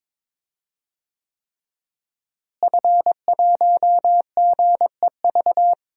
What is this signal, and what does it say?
Morse code, heard as a keyed tone, about 22 wpm, 700 Hz. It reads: F1GEV
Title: DXCC-100 Call Signs - 5 Characters - Repeated 3x - 22wpm